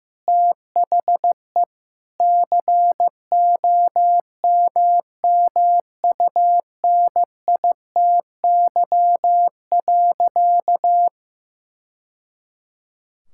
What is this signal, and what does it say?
Morse code, heard as a keyed tone, about 15 wpm, 700 Hz. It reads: THE COMMUNITY.
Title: Sentences from Top 1000 Words - Review Part 5 - 15wpm